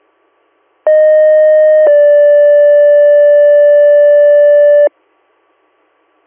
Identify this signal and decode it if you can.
Two-tone page, heard as a one-second tone, then a three-second tone, about 620 then 580 Hz. It is heard from 0.9 to 4.9 s.